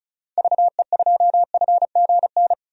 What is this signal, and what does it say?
Morse code, heard as a keyed tone, about 34 wpm, 700 Hz. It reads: VE2FGN
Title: DXCC-100 Call Signs - 6 Characters - 35wpm